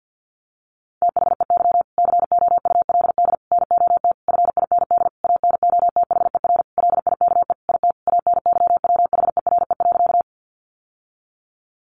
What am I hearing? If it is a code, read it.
Morse code, heard as a keyed tone, about 50 wpm, 700 Hz. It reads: THEY COULD NOT FIND ANOTHER LIKE IT ANYWHERE.
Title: Sentences from Top 1400 Words 50wpm